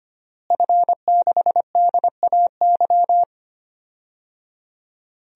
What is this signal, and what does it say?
Morse code, heard as a keyed tone, about 25 wpm, 700 Hz. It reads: F6DAY
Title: DXCC-100 Call Signs - 5 Characters - Repeated 3x - 25wpm